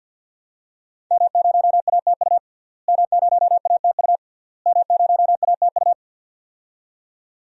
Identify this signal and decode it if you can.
Morse code, heard as a keyed tone, about 51 wpm, 700 Hz. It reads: M0ATU M0ATU M0ATU
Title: DXCC-100 Call Signs - 5 Characters - Repeated 3x - 50wpm